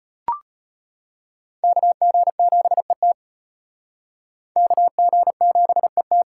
Morse code, heard as a keyed tone, about 34 wpm, 700 Hz. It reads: KG7ET KG7ET